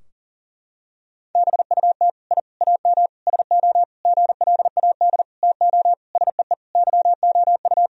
Morse code, heard as a keyed tone, about 37 wpm, 700 Hz. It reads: BUT I AM SO GLAD TO SEE YOU
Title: Sentences from Top 1000 Words - Review Part 2 - 40wpm